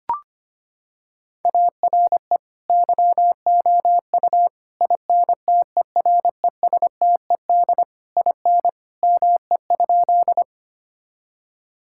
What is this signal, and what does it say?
Morse code, heard as a keyed tone, about 25 wpm, 700 Hz. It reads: ARE YOU INTERESTED IN ME?
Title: Sentences from Top 1000 Words - Review Part 6 - 25wpm